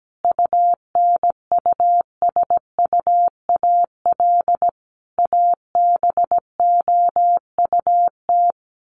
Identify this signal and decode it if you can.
Morse code, heard as a keyed tone, about 17 wpm, 700 Hz. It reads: UNUSUAL ABOUT
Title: Sentences from Top 1400 Words 17wpm